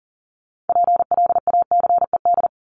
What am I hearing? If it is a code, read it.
Morse code, heard as a keyed tone, about 40 wpm, 700 Hz. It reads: PLACED
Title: ED Suffix - Top 100 Words with Base Words 40wpm